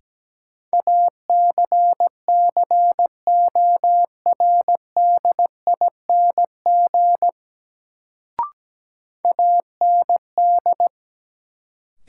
Morse code, heard as a keyed tone, about 17 wpm, 700 Hz. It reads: ACCORDING AND
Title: Sets of 2 Words - New Word Drill - 17wpm